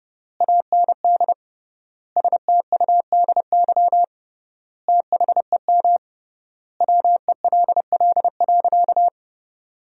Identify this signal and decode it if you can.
Morse code, heard as a keyed tone, about 30 wpm, 700 Hz. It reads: AND STUDY THEM WELL.
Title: 3x Word-Spacing - Sentences from Top 300 Words - 30wpm